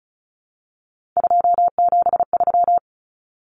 Morse code, heard as a keyed tone, about 34 wpm, 700 Hz. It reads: 273
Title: Numbers - 3-Digit 35wpm